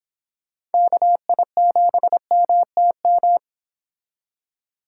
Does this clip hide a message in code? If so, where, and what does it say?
Yes: Morse code from 0.7 to 3.4 s, heard as a keyed tone, about 28 wpm, 700 Hz. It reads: KI7MTM